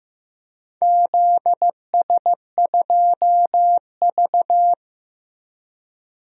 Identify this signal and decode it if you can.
Morse code, heard as a keyed tone, about 15 wpm, 700 Hz. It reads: ZS2V
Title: DXCC-100 Call Signs - 4 Characters - 15wpm